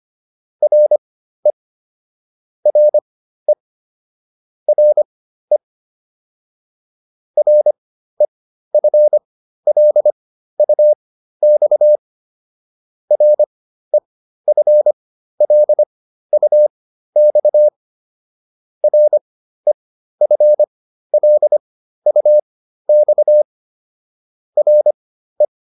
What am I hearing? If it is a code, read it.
Morse code, heard as a keyed tone, about 25 wpm, spaced as for 13 wpm, 600 Hz. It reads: RE RE RE REFLUX REFLUX REFLUX RE